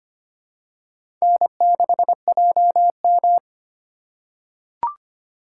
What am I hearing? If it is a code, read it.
Morse code, heard as a keyed tone, about 25 wpm, 700 Hz. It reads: N6JM
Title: USA Call Signs - 4 Characters - 2x Character Spacing - 25wpm